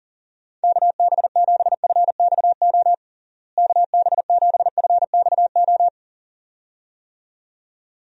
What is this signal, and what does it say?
Morse code, heard as a keyed tone, about 40 wpm, 700 Hz. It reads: KB7FXO KB7FXO
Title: USA Call Signs - 6 Characters - Repeated 2x - 40wpm